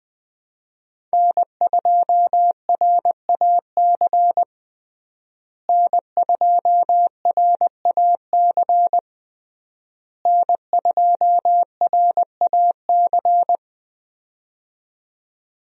Morse code, heard as a keyed tone, about 20 wpm, 700 Hz. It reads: N2RAC N2RAC N2RAC
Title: USA Call Signs - 5 Characters - Repeated 3x - 20wpm